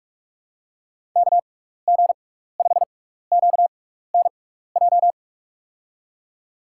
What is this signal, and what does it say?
Morse code, heard as a keyed tone, about 46 wpm, spaced as for 16 wpm, 700 Hz. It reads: KG5QNJ
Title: USA Call Signs - 6 Characters - 6x Character Spacing - 45wpm